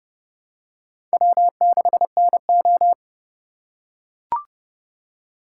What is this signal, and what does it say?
Morse code, heard as a keyed tone, about 29 wpm, 700 Hz. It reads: W6NO
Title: USA Call Signs - 4 Characters - 4x Character Spacing - 30wpm